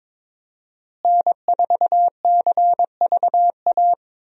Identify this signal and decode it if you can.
Morse code, heard as a keyed tone, about 22 wpm, 700 Hz. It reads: N4CVA
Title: USA Call Signs - 5 Characters - Repeated 3x - 22wpm